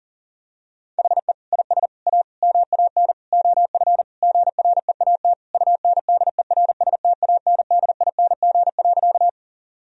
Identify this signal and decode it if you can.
Morse code, heard as a keyed tone, about 40 wpm, 700 Hz. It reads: HE IS A MAN OF GREAT UNDERSTANDING.
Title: Sentences from Top 1000 Words - Review Part 3 - 40wpm